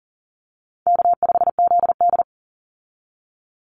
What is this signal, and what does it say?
Morse code, heard as a keyed tone, about 40 wpm, 700 Hz. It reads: K5ZD
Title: Easy Call Signs - 40wpm